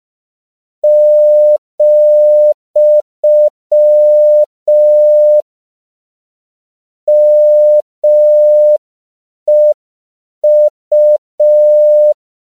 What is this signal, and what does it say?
Morse code, heard as a keyed tone, about 5 wpm, 600 Hz. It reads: , MEU
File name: CW / 5PPM